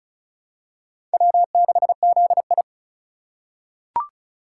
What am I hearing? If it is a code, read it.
Morse code, heard as a keyed tone, about 36 wpm, 700 Hz. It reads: W6ZI
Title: USA Call Signs - 4 Characters - 4x Character Spacing - 35wpm